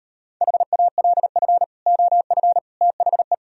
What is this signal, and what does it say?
Morse code, heard as a keyed tone, about 38 wpm, 700 Hz. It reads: HALF OF THE